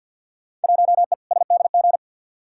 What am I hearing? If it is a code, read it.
Morse code, heard as a keyed tone, about 47 wpm, 700 Hz. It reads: 1E SDG
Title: Field Day Exchanges - 50wpm